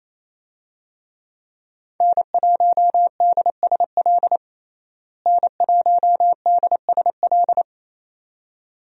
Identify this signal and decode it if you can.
Morse code, heard as a keyed tone, about 27 wpm, 700 Hz. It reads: N1DSL N1DSL
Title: USA Call Signs - 5 Characters - Repeated 2x - 28wpm